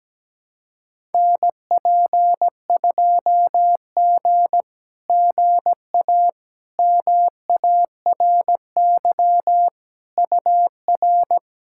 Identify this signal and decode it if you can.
Morse code, heard as a keyed tone, about 17 wpm, 700 Hz. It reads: NP2G GA MARY UR
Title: SOTA QSO Practice 17wpm